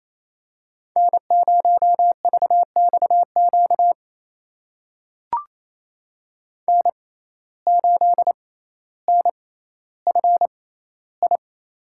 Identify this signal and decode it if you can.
Morse code, heard as a keyed tone, about 28 wpm, spaced as for 14 wpm, 700 Hz. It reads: N0VXQ N8NFI
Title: USA Call Signs - 5 Characters - 6x Character Spacing - 28wpm